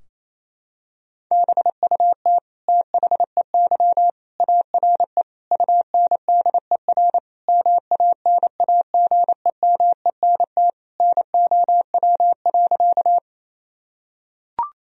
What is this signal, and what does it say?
Morse code, heard as a keyed tone, about 27 wpm, 700 Hz. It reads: BUT THEY ARE UNDER MANAGEMENT NOW.
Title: Sentences from Top 1000 Words - Review Part 4 - 28wpm